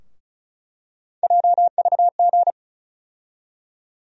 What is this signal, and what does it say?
Morse code, heard as a keyed tone, about 34 wpm, 700 Hz. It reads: JVG
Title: USA Call Sign Postfixes - 3 Letters 35wpm